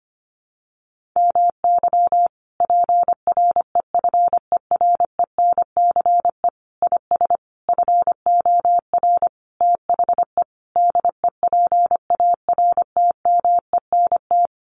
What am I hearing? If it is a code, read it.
Morse code, heard as a keyed tone, about 25 wpm, 700 Hz. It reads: MY PREFERENCE IS FOR THE DEPARTMENT
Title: Sentences from Top 1500 Words 25wpm